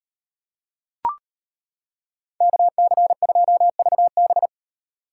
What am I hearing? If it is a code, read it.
Morse code, heard as a keyed tone, about 33 wpm, 700 Hz. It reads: KC2VB